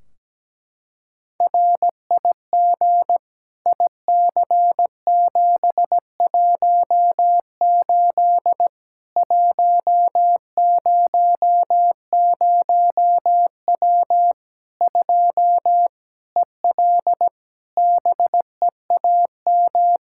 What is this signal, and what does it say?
Morse code, heard as a keyed tone, about 17 wpm, 700 Hz. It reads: RIG IC718 100W 2 EL BEAM
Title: Practice QSO - Exchange 2 - Info - 17wpm